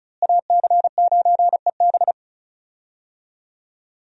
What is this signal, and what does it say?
Morse code, heard as a keyed tone, about 36 wpm, 700 Hz. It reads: AC9EB
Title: USA Call Signs - 5 Characters - Repeated 3x - 35wpm